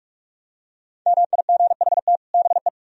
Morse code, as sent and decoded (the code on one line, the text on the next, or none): -- .. --. .... - / -... .
MIGHT BE